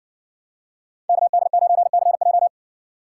Morse code, heard as a keyed tone, about 60 wpm, 700 Hz. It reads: KD9CJ